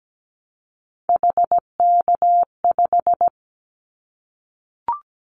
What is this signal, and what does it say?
Morse code, heard as a keyed tone, about 17 wpm, 700 Hz. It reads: HK5
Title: Callsign Prefixes for Top 100 DXCC - 17wpm